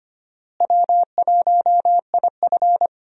Morse code, heard as a keyed tone, about 25 wpm, 700 Hz. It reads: W1IF
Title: USA Call Signs - 4 Characters - Repeated 3x - 25wpm